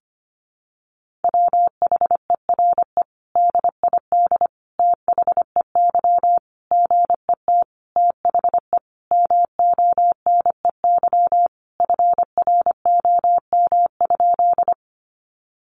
Morse code, heard as a keyed tone, about 25 wpm, 700 Hz. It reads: WHERE DID THEY GET THE MONEY FROM?
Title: Sentences from Top 400 Words 25wpm